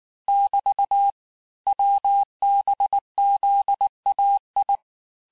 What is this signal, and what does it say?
Morse code, heard as a keyed tone, about 19 wpm, 800 Hz. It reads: <BT> WBZAI
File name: cours-du-lundi files / cours-1595-teste-CHIFFRE-vitesse18-MPM